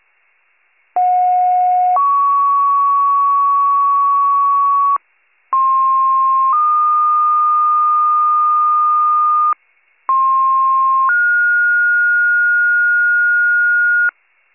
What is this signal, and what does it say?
Two-tone page, heard as a one-second tone, then a three-second tone, about 730 then 1120 Hz, about 1060 then 1220 Hz, about 1060 then 1430 Hz. It is heard from 1.0 to 14.1 s.
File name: pager / 98 FIRE / wcalert